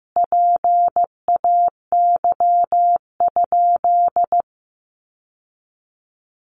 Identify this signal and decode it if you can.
Morse code, heard as a keyed tone, about 15 wpm, 700 Hz. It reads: PAY?
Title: Sentences from Top 700 Words 15wpm